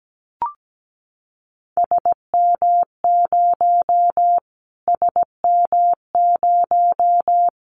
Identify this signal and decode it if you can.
Morse code, heard as a keyed tone, about 17 wpm, 700 Hz. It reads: SM0 SM0
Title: Callsign Prefixes for Top 25 DXCC - Repeated 3x - 17wpm